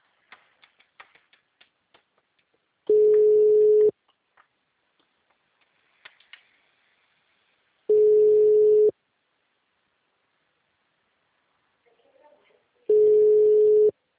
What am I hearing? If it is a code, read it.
Ringback tone (Europe, 425 Hz).